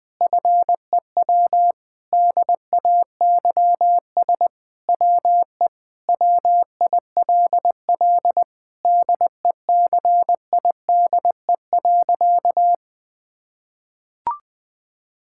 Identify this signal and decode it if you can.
Morse code, heard as a keyed tone, about 20 wpm, 700 Hz. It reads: FEW DAYS WE WILL DECIDE.
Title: Sentences with 7-Word Phrases - Spoken Beginning - 20wpm